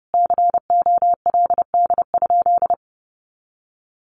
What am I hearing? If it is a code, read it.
Morse code, heard as a keyed tone, about 30 wpm, 700 Hz. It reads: COLD?
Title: Sentences from Top 400 Words 30wpm